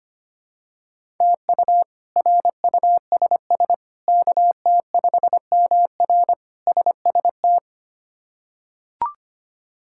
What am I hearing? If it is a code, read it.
Morse code, heard as a keyed tone, about 25 wpm, 700 Hz. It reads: TU RUSS KT5MR SST
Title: K1USN SST QSO Practice - 25wpm